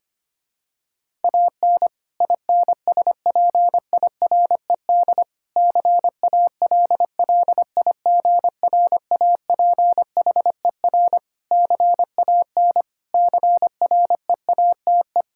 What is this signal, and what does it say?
Morse code, heard as a keyed tone, about 25 wpm, 700 Hz. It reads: AN INSPIRED CALLIGRAPHER CAN CREATE
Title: Pangram Challenge v2 - 25wpm